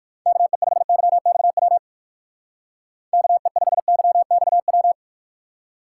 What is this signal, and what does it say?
Morse code, heard as a keyed tone, about 56 wpm, 700 Hz. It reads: KE5YXW KE5YXW